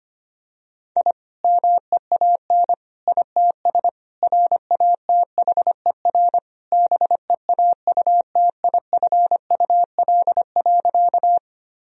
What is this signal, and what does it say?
Morse code, heard as a keyed tone, about 25 wpm, 700 Hz. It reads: I MEAN ITS RATHER BEAUTIFUL.
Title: Sentences from Top 1000 Words - Review Part 4 - 25wpm